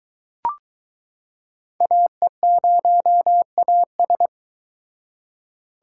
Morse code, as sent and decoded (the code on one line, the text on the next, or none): .- . ----- .- ...
AE0AS